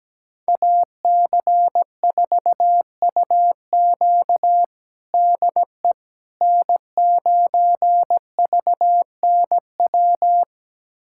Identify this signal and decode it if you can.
Morse code, heard as a keyed tone, about 17 wpm, 700 Hz. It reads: AC4UQ DE N9VNW
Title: USA Callsign DE Callsign - 5 Characters - 17wpm